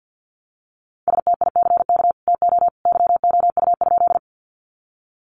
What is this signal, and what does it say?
Morse code, heard as a keyed tone, about 51 wpm, 700 Hz. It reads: STICK TO YOU?
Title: Sentences from Top 1200 Words 50wpm